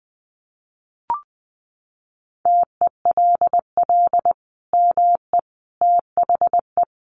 Morse code, heard as a keyed tone, about 20 wpm, 700 Hz. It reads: TELL ME THE
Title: Sentences from Top 800 Words 20wpm